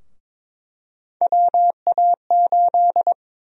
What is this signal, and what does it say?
Morse code, heard as a keyed tone, about 22 wpm, 700 Hz. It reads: WA8
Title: USA Call Sign Prefixes - Repeated 2x - 22wpm